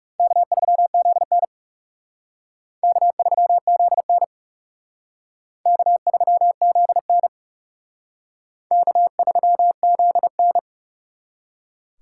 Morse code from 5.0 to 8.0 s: K3ZN

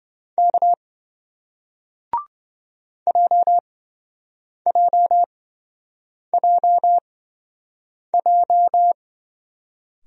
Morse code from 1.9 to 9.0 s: JJJJ

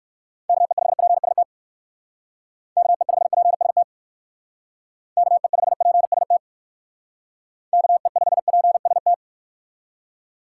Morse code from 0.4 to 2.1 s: KE5PST

